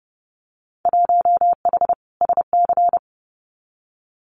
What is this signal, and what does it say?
Morse code, heard as a keyed tone, about 30 wpm, 700 Hz. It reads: 1H SC
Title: Winter Field Day Exchanges 30wpm